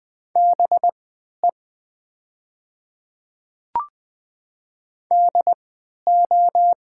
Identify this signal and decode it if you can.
Morse code, heard as a keyed tone, about 20 wpm, spaced as for 11 wpm, 700 Hz. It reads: BE DO
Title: 3x Character Spacing - Top 200 Words - 20wpm